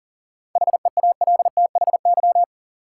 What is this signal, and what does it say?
Morse code, heard as a keyed tone, about 40 wpm, 700 Hz. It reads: HEALTHY
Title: Sentences from Top 1000 Words - Review Part 5 - New Word Drill - 40wpm